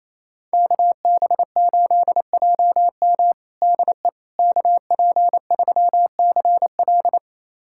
Morse code, heard as a keyed tone, about 27 wpm, 700 Hz. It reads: KB8JM DE KP3CL
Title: USA Callsign DE Callsign - 5 Characters - 28wpm